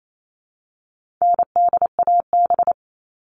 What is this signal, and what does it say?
Morse code, heard as a keyed tone, about 28 wpm, 700 Hz. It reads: NDAB
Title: All Cut Numbers - 4-Digit 28wpm